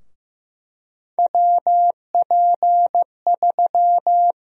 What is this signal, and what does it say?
Morse code, heard as a keyed tone, about 15 wpm, 700 Hz. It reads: WP3
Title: USA Call Sign Prefixes - 15wpm